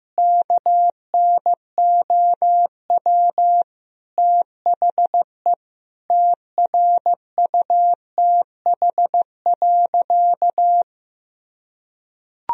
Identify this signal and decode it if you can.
Morse code, heard as a keyed tone, about 15 wpm, 700 Hz. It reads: KNOW THE TRUTH.
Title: Sentences from Top 1000 Words - Review Part 3 - 15wpm